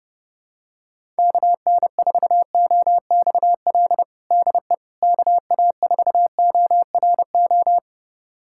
Morse code, heard as a keyed tone, about 30 wpm, 700 Hz. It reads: KN4OXL DE KA4ORO
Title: USA Callsign DE Callsign 30wpm